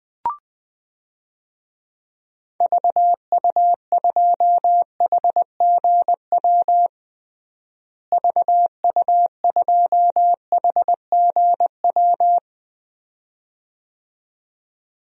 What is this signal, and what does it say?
Morse code, heard as a keyed tone, about 21 wpm, 700 Hz. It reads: VU2HGW VU2HGW
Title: DXCC-100 Call Signs - 6 Characters - Repeated 2x - 20wpm